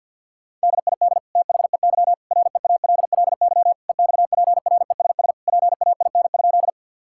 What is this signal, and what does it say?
Morse code, heard as a keyed tone, about 50 wpm, 700 Hz. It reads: DID THEY REALLY EXPRESS PAIN?
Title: Sentences from Top 1600 Words 50wpm